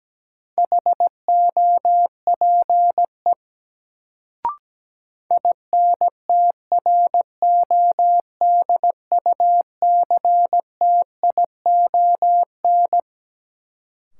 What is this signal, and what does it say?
Morse code, heard as a keyed tone, about 17 wpm, 700 Hz. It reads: HOPE INTRODUCTION
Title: Sentences from Top 1000 Words - New Word Drill - 17wpm